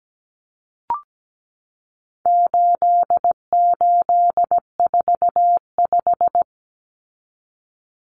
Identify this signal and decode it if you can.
Morse code, heard as a keyed tone, about 17 wpm, 700 Hz. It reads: 8845
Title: Numbers - 4-Digit 17wpm